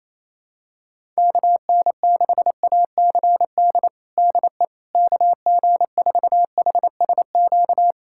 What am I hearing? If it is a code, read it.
Morse code, heard as a keyed tone, about 28 wpm, 700 Hz. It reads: KN6ACD DE KG4HSQ
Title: USA Callsign DE Callsign 28wpm